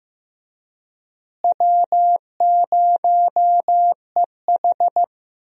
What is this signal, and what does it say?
Morse code, heard as a keyed tone, about 15 wpm, 700 Hz. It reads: W0EH
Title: USA Call Signs - 4 Characters - Repeated 3x - 15wpm